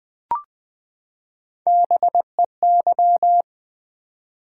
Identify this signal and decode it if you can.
Morse code, heard as a keyed tone, about 20 wpm, 700 Hz. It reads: BEY
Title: Build A Word - 3 Letters At A Time - 6-Letter Words - 20wpm